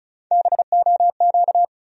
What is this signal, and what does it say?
Morse code, heard as a keyed tone, about 34 wpm, 700 Hz. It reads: BOQ
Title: USA Call Sign Postfixes - 3 Letters 35wpm